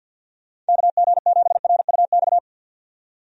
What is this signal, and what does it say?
Morse code, heard as a keyed tone, about 52 wpm, 700 Hz. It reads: KG7RUX
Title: USA Call Signs - 6 Characters - 50wpm